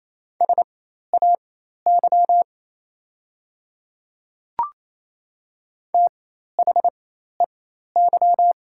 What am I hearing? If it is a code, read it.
Morse code, heard as a keyed tone, about 27 wpm, spaced as for 13 wpm, 700 Hz. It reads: SAY THEY
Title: 4x Character Spacing - Top 500 Words - 28wpm